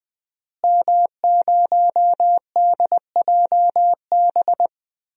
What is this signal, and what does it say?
Morse code, heard as a keyed tone, about 20 wpm, 700 Hz. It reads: M0DJB
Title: DXCC-100 Call Signs - 5 Characters - Repeated 3x - 20wpm